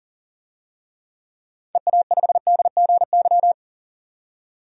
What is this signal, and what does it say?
Morse code, heard as a keyed tone, about 40 wpm, 700 Hz. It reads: EA5DGY
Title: DXCC-100 Call Signs - 6 Characters - 40wpm